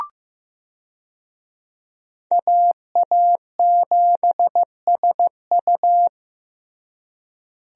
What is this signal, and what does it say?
Morse code, heard as a keyed tone, about 15 wpm, 700 Hz. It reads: AA7SU
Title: USA Call Signs - 5 Characters - Repeated 3x - 15wpm